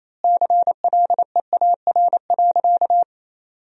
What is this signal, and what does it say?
Morse code, heard as a keyed tone, about 28 wpm, 700 Hz. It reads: CLEAR.
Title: 3x Word-Spacing - Sentences from Top 500 Words - 28wpm